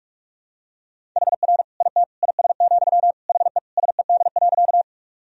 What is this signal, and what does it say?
Morse code, heard as a keyed tone, about 46 wpm, 700 Hz. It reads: HR IT IS, HE SED.
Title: Sentences from Top 200 Words and Abbreviations 45wpm